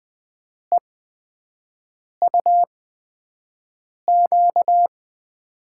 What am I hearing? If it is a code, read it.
Morse code, heard as a keyed tone, about 20 wpm, spaced as for 5 wpm, 700 Hz. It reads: EUQ
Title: USA Call Signs - 5 Characters - 8x Character Spacing - 20wpm